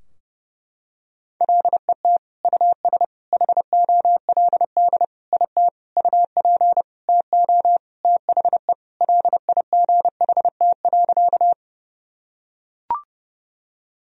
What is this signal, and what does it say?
Morse code, heard as a keyed tone, about 31 wpm, 700 Hz. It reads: LET US HOLD IT UP TO THE LIGHT.
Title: Sentences from Top 1000 Words - Review Part 2 - 30wpm